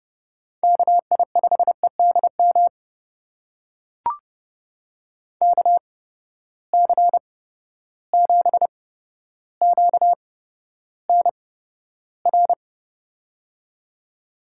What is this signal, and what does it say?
Morse code, heard as a keyed tone, about 32 wpm, spaced as for 13 wpm, 700 Hz. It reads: KI5EDM KC7QNR
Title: USA Call Signs - 6 Characters - 8x Character Spacing - 30wpm